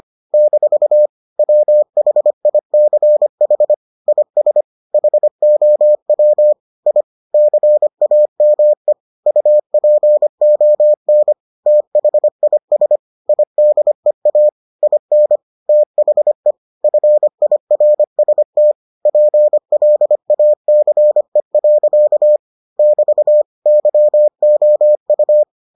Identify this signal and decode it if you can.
Morse code, heard as a keyed tone, about 25 wpm, 600 Hz. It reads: - WHICH IS HOW I CAME UPON THIS IDEA IN THE FIRST PLACE. <BT> YOU